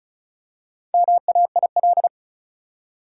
Morse code, heard as a keyed tone, about 34 wpm, 700 Hz. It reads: MAIL
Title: Four Letter Words - 35wpm